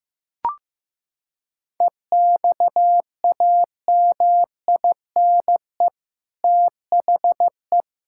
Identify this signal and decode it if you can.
Morse code, heard as a keyed tone, about 15 wpm, 700 Hz. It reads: EXAMINE THE
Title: Sentences from Top 1600 Words 15wpm